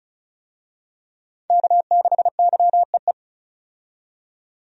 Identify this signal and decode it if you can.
Morse code, heard as a keyed tone, about 36 wpm, 700 Hz. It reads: K6YEE
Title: USA Call Signs - 5 Characters - 35wpm